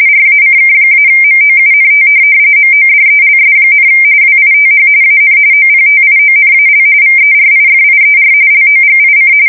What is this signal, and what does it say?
RTTY, about 45.45 baud. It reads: RACK-MOUNT MODELS, WITH AN OPTIONAL VERIFIER LOGIC
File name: fsk170 / teletype-corp